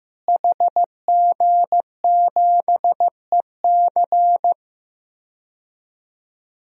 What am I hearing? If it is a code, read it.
Morse code, heard as a keyed tone, about 15 wpm, 700 Hz. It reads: HG7EC
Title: DXCC-100 Call Signs - 5 Characters - Repeated 3x - 15wpm